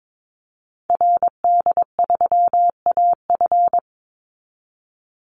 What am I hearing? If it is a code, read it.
Morse code, heard as a keyed tone, about 22 wpm, 700 Hz. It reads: RD3AF
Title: Call Signs Repeated 2x - 22wpm